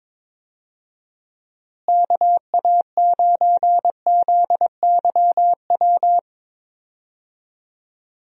Morse code, heard as a keyed tone, about 22 wpm, 700 Hz. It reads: KA9ZYW
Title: USA Call Signs - 6 Characters - Repeated 3x - 22wpm